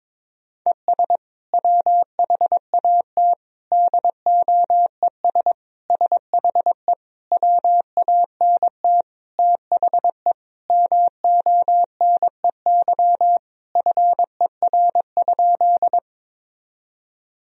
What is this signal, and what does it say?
Morse code, heard as a keyed tone, about 22 wpm, 700 Hz. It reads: ES WHAT DOES SHE WANT THE MONEY FER?
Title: Sentences from Top 400 Words and Abbreviations 22wpm